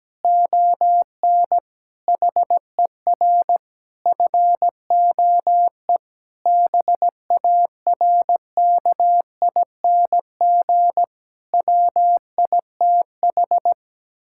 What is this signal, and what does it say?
Morse code, heard as a keyed tone, about 17 wpm, 700 Hz. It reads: ON HER FOE BARKING WITH
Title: Pangram Challenge v2 - 17wpm